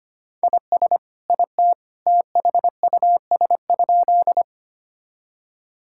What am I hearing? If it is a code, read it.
Morse code, heard as a keyed tone, about 25 wpm, 700 Hz. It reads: IS IT THUS?
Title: Sentences from Top 1000 Words - Review Part 5 - 25wpm